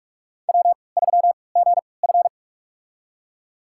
Morse code, heard as a keyed tone, about 46 wpm, spaced as for 25 wpm, 700 Hz. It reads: W3GF
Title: USA Call Signs - 4 Characters - 3x Character Spacing - 45wpm